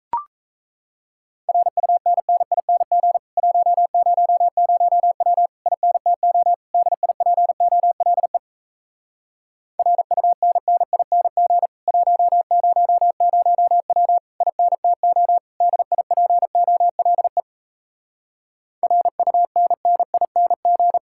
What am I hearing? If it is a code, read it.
Morse code, heard as a keyed tone, about 37 wpm, 700 Hz. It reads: RUNNING 100W INTO DIPOLE RUNNING 100W INTO DIPOLE RUNNING